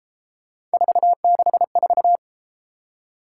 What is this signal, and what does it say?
Morse code, heard as a keyed tone, about 32 wpm, 700 Hz. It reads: 464